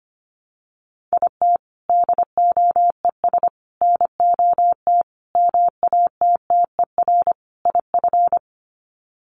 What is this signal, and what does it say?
Morse code, heard as a keyed tone, about 25 wpm, 700 Hz. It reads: IT DOES NOT MATTER IF
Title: Sets of 5 Words Encore 25wpm